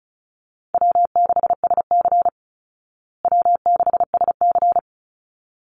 Morse code, heard as a keyed tone, about 34 wpm, 700 Hz. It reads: W6SC W6SC
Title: Easy Call Signs Repeated 2x - 35wpm